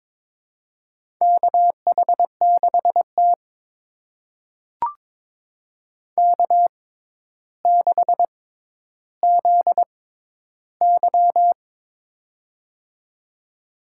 Morse code, heard as a keyed tone, about 22 wpm, spaced as for 11 wpm, 700 Hz. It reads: KH6T K6ZY
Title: USA Call Signs - 4 Characters - 6x Character Spacing - 22wpm